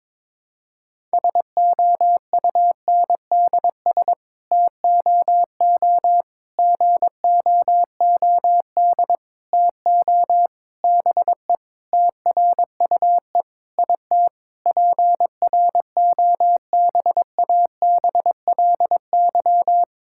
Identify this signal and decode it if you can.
Morse code, heard as a keyed tone, about 22 wpm, 700 Hz. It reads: SOUNDS TOO GOOD TO BE TRUE IT PROBABLY
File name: Sets of 9 Words - 22wpm